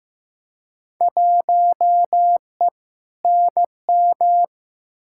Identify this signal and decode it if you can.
Morse code, heard as a keyed tone, about 15 wpm, 700 Hz. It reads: 1E NM
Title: Field Day Exchanges - 15wpm